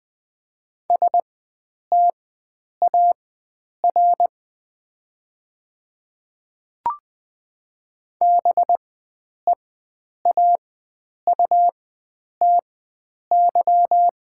Morse code, heard as a keyed tone, about 20 wpm, spaced as for 9 wpm, 700 Hz. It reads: STAR BEAUTY
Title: 4x Character Spacing - Top 500 Words - 20wpm